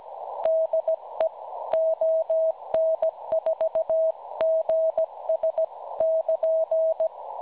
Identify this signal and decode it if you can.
Morse code, heard as a keyed tone, about 17 wpm, 640 Hz. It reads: DE ON4GS <KN>
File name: cwronde / websdr recording .6kHz